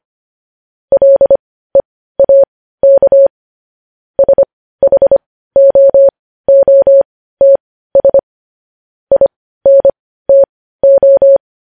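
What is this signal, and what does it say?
Morse code, heard as a keyed tone, about 25 wpm, spaced as for 15 wpm, 550 Hz. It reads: LEAK SHOOTS INTO
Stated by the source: Morse code news headlines at 15 wpm.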